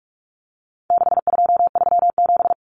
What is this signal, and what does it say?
Morse code, heard as a keyed tone, about 44 wpm, 700 Hz. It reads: 6237
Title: Numbers - 4-Digit 45wpm